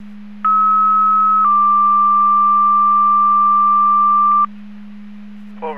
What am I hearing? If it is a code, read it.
Two-tone page, heard as a one-second tone, then a three-second tone, about 1290 then 1180 Hz. It is heard from 0.4 to 4.4 s.
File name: Portage 154.385 Paging